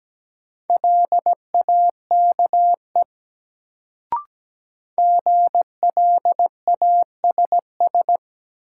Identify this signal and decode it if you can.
Morse code, heard as a keyed tone, about 17 wpm, 700 Hz. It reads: LAKE GLASS
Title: Sentences from Top 1000 Words - Review Part 3 - New Word Drill - 17wpm